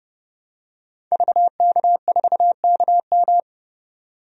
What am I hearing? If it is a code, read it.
Morse code, heard as a keyed tone, about 30 wpm, 700 Hz. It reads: VK4KM
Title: DXCC-100 Call Signs - 5 Characters - 30wpm